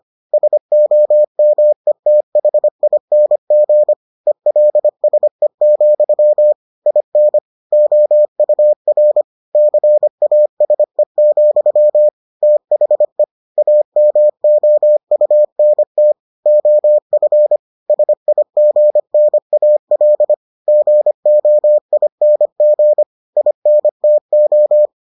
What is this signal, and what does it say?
Morse code, heard as a keyed tone, about 25 wpm, 600 Hz. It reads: SOMETHING ELSE, IN OUR CASE, THE AMOUNT OF SIGNAL GOING INTO